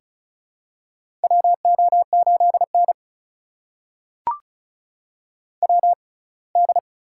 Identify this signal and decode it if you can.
Morse code, heard as a keyed tone, about 36 wpm, spaced as for 24 wpm, 700 Hz. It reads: WO8N WD
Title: USA Call Signs - 6x Character Spacing - 35wpm